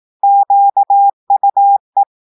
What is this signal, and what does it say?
Morse code, heard as a keyed tone, about 18 wpm, 800 Hz. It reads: QUE
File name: audios / portugais / portugais texte en clair 1 vitesse 18 mots minutes